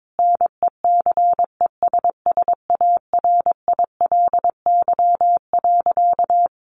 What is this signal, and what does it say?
Morse code, heard as a keyed tone, about 22 wpm, 700 Hz. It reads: NECESSARILY.
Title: Sentences from Top 1400 Words 22wpm